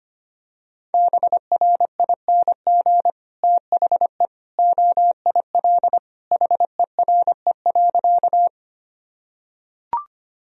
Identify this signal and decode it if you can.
Morse code, heard as a keyed tone, about 25 wpm, 700 Hz. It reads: BRING THE OIL HERE.
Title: Sentences from Top 1000 Words - Review Part 4 - 25wpm